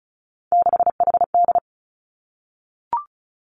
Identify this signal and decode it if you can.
Morse code, heard as a keyed tone, about 36 wpm, 700 Hz. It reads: 6HD
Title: Three Challenging Letter-Number Combinations 35wpm